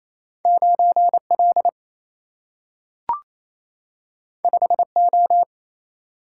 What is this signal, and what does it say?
Morse code, heard as a keyed tone, about 28 wpm, 700 Hz. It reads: 9 L 5 O
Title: Single-Letter Single-Number 28wpm